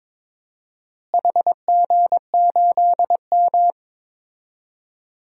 Morse code, heard as a keyed tone, about 22 wpm, 700 Hz. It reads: HG8M
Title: DXCC-100 Call Signs - 4 Characters - Repeated 3x - 22wpm